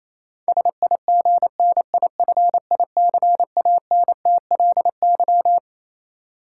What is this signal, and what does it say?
Morse code, heard as a keyed tone, about 28 wpm, 700 Hz. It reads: SIGNIFICANTLY
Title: Sentences from Top 1500 Words - New Word Drill - 28wpm